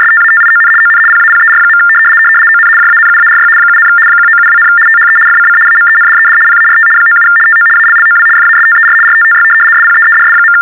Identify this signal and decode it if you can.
RTTY, about 75 baud. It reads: STATION ON PAPER. ON JANUARY 7, 1972, THE FCC AMENDED PART 97 TO ALLOW FASTER RTTY SPEEDS.
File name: teletype-RTTY-75